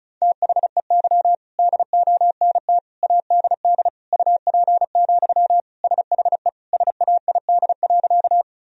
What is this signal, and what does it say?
Morse code, heard as a keyed tone, about 36 wpm, 700 Hz. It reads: THEY DONT ADD UP, SHE SAID.
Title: Sentences from Top 1000 Words - Review Part 1 - 35wpm